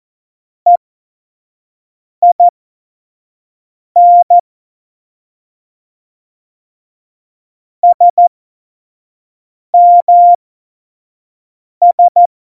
Morse code, heard as a keyed tone, about 14 wpm, spaced as for 5 wpm, 700 Hz. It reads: EIN SMS
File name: Morsekurs / Witze / Witz 28 April 2020, 25 F70